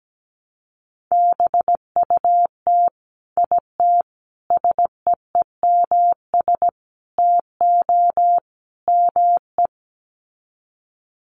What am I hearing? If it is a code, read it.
Morse code, heard as a keyed tone, about 17 wpm, 700 Hz. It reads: BUT IT SEEMS TO ME
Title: Sets of 5 Words Encore2 17wpm